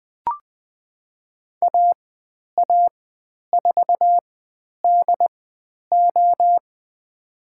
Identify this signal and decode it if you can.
Morse code, heard as a keyed tone, about 20 wpm, spaced as for 10 wpm, 700 Hz. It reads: AA4DO
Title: 102 Lesson 17g - Letter u - Call Signs 20wpm